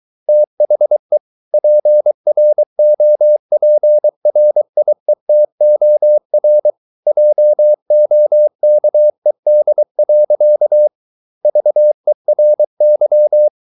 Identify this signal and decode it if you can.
Morse code, heard as a keyed tone, about 23 wpm, 600 Hz. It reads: THE PROPRIETOR JOKED. VERY